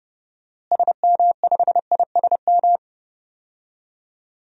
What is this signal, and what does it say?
Morse code, heard as a keyed tone, about 29 wpm, 700 Hz. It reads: SM5ISM
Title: DXCC-100 Call Signs - 6 Characters - Repeated 3x - 30wpm